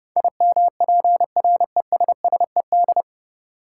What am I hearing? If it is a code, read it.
Morse code, heard as a keyed tone, about 30 wpm, 700 Hz. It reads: IMPRESSED
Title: Sentences from Top 1600 Words - New Word Drill - 30wpm